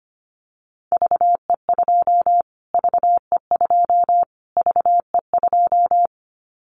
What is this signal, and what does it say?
Morse code, heard as a keyed tone, about 25 wpm, 700 Hz. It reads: VE2 VE2 VE2
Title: Callsign Prefixes for Top 25 DXCC - Repeated 3x - 25wpm